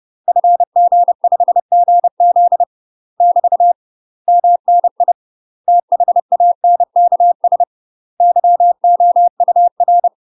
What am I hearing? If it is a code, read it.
Morse code, heard as a keyed tone, about 30 wpm, 700 Hz. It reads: FG5GZ <BT> MNI THANKS YOUR
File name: UK MASTER QSOs TRAINING / 30WPM / UK MASTER QSOs 0023